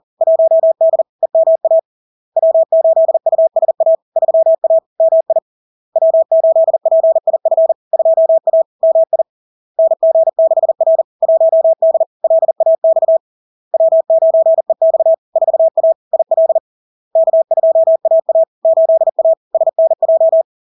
Morse code, heard as a keyed tone, about 40 wpm, 650 Hz. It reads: 1D EMA W8USA 3A MI W8PIF 2A MI NG6R 1D LAX W9EX 4A IL K2AA 8A SNJ
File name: LARK-FD-2022-Log-in-CW-40wpm-650-Hz